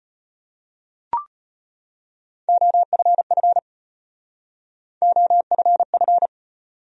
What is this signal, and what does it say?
Morse code, heard as a keyed tone, about 33 wpm, 700 Hz. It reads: OFF OFF